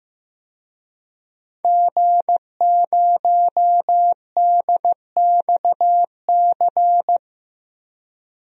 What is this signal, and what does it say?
Morse code, heard as a keyed tone, about 16 wpm, 700 Hz. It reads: G0DXC
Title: DXCC-100 Call Signs - 5 Characters - 15wpm